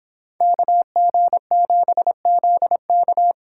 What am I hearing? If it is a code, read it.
Morse code, heard as a keyed tone, about 27 wpm, 700 Hz. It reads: KG7ZK
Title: Speed Racing - USA 2x2 Call Signs 22wpm